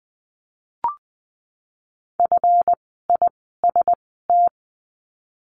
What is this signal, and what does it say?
Morse code, heard as a keyed tone, about 20 wpm, spaced as for 15 wpm, 700 Hz. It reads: FIST
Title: 139 Lesson 23f - Letter f - Words 20wpm